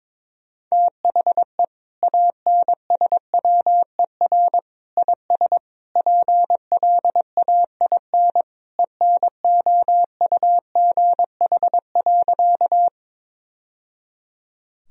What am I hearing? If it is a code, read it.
Morse code, heard as a keyed tone, about 22 wpm, 700 Hz. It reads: THE ANSWER IS PLAIN ENOUGH.
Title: Sentences from Top 1000 Words - Review Part 2 - 22wpm